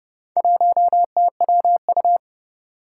Morse code, heard as a keyed tone, about 30 wpm, 700 Hz. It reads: 1TWU
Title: Four Challenging Letter-Number Combinations 30wpm